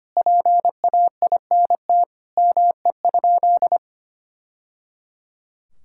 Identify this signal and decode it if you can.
Morse code, heard as a keyed tone, about 25 wpm, 700 Hz. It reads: PAINT ME?
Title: Sentences from Top 1000 Words - Review Part 6 - 25wpm